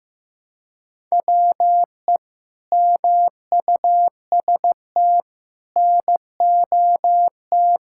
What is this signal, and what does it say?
Morse code, heard as a keyed tone, about 15 wpm, 700 Hz. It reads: WE MUST NOT
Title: Sentences from Top 1600 Words 15wpm